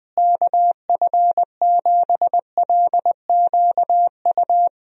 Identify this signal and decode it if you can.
Morse code, heard as a keyed tone, about 20 wpm, 700 Hz. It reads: KF7LQU
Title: USA Call Signs - 6 Characters - Repeated 2x - 20wpm